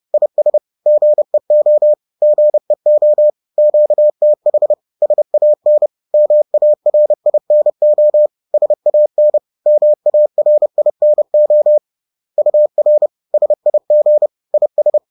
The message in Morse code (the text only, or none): IS GEO GEO QTH SAN MARINO SAN MARINO UR SIG IS